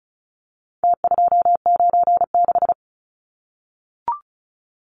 Morse code, heard as a keyed tone, about 34 wpm, 700 Hz. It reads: T296
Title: Numbers - Contest Style - 4-Digit 35wpm